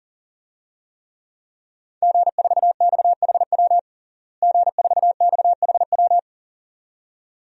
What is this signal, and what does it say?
Morse code, heard as a keyed tone, about 40 wpm, 700 Hz. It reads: G4XHW G4XHW
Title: DXCC-100 Call Signs - 5 Characters - Repeated 2x - 40wpm